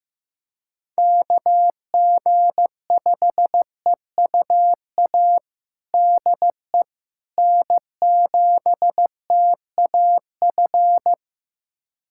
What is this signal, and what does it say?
Morse code, heard as a keyed tone, about 16 wpm, 700 Hz. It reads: KG5EUA DE N7TAF
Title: USA Callsign DE Callsign 15wpm